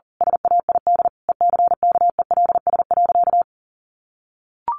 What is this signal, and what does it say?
Morse code, heard as a keyed tone, about 40 wpm, 700 Hz. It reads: SAID ECKELS.
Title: A Sound Of Thunder 40wpm